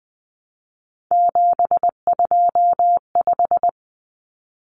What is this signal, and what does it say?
Morse code, heard as a keyed tone, about 20 wpm, 700 Hz. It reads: 725
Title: Common Cut Numbers - 3-Digit 20wpm